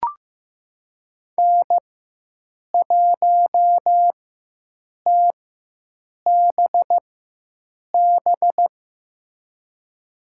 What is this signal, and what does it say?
Morse code, heard as a keyed tone, about 15 wpm, spaced as for 7 wpm, 700 Hz. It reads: N1TBB
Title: USA Call Signs - 5 Characters - 4x Character Spacing - 15wpm